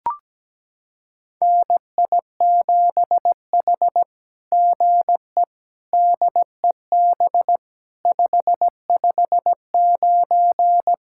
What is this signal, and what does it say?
Morse code, heard as a keyed tone, about 17 wpm, 700 Hz. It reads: NI7H GE DEB 559
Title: SOTA QSO Practice 17wpm